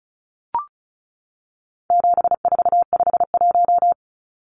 Morse code, heard as a keyed tone, about 34 wpm, 700 Hz. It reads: 7451
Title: Numbers - 4-Digit 35wpm